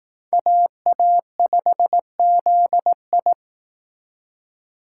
Morse code, heard as a keyed tone, about 18 wpm, 700 Hz. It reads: AA5ZI
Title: Speed Racing - USA 2x2 Call Signs 15wpm